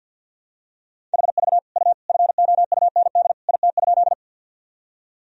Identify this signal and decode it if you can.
Morse code, heard as a keyed tone, about 50 wpm, 700 Hz. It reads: HV U FOUND IT?
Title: Sentences from Top 300 Words and Abbreviations 50wpm